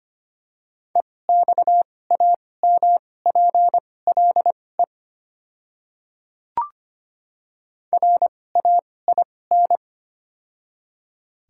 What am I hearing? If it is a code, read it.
Morse code, heard as a keyed tone, about 25 wpm, spaced as for 18 wpm, 700 Hz. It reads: EXAMPLE RAIN
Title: 2x Character Spacing - Top 300 Words - 25wpm